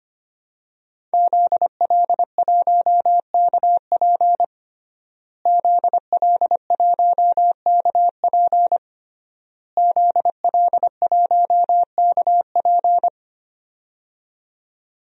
Morse code, heard as a keyed tone, about 25 wpm, 700 Hz. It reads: ZL1KP ZL1KP ZL1KP
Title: DXCC-100 Call Signs - 5 Characters - Repeated 3x - 25wpm